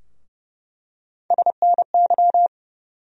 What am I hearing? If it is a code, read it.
Morse code, heard as a keyed tone, about 30 wpm, 700 Hz. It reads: SNY
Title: USA Call Sign Postfixes - 3 Letters 30wpm